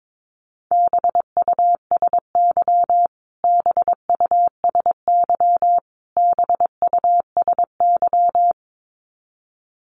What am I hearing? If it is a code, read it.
Morse code, heard as a keyed tone, about 22 wpm, 700 Hz. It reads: BUSY BUSY BUSY
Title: Top 500 Words Repeated 22wpm